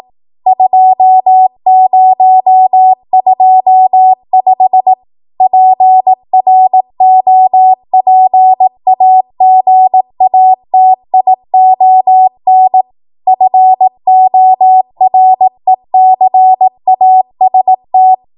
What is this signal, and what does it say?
Morse code, heard as a keyed tone, about 18 wpm, 750 Hz. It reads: 2025 PROPAGATION FORECAST
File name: Morse / 18WPM